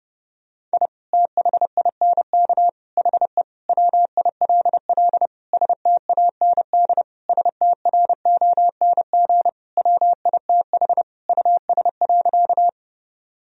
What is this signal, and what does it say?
Morse code, heard as a keyed tone, about 30 wpm, 700 Hz. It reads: I THINK HE WILL STAND STRONG WITH US.
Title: Sentences from Top 1000 Words - Review Part 3 - 30wpm